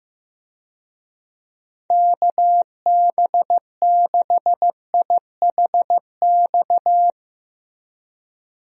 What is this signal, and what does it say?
Morse code, heard as a keyed tone, about 16 wpm, 700 Hz. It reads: KB6IHX
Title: USA Call Signs - 6 Characters - Repeated 3x - 15wpm